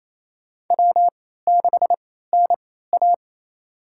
Morse code, heard as a keyed tone, about 28 wpm, spaced as for 16 wpm, 700 Hz. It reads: W6NA
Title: USA Call Signs - 4 Characters - 3x Character Spacing - 28wpm